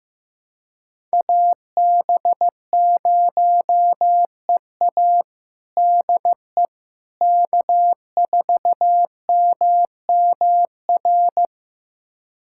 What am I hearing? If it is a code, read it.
Morse code, heard as a keyed tone, about 16 wpm, 700 Hz. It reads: AB0EA DE K4MMR
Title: USA Callsign DE Callsign - 5 Characters - 15wpm